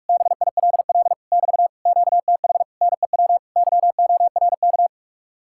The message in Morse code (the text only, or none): BILL <BT> QTH NEW YORK